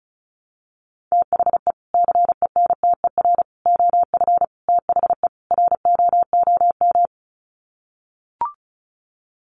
Morse code, heard as a keyed tone, about 34 wpm, 700 Hz. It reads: THE CENTER OF THE ROOM
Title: Sets of 5 Words Encore 35wpm